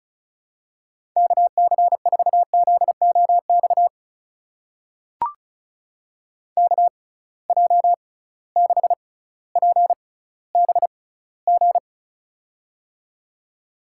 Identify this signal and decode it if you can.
Morse code, heard as a keyed tone, about 36 wpm, spaced as for 18 wpm, 700 Hz. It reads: KC4ZOX KJ6PBG
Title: USA Call Signs - 6 Characters - 6x Character Spacing - 35wpm